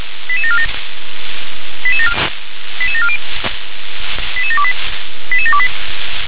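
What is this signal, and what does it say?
Five-tone selective call (ZVEI), heard as quick runs of five tones, tones of about 70 ms. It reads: 88747 88740 88728 88728
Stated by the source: ZVEI five-tone call received by SDR.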